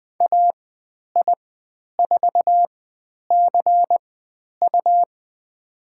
Morse code, heard as a keyed tone, about 20 wpm, spaced as for 10 wpm, 700 Hz. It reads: AI4CU
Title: 102 Lesson 17g - Letter u - Call Signs 20wpm